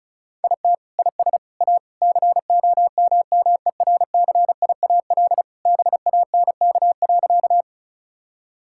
Morse code, heard as a keyed tone, about 36 wpm, 700 Hz. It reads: IT IS A COMMERCIAL BANK.
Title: Sentences from Top 1000 Words - Review Part 6 - 35wpm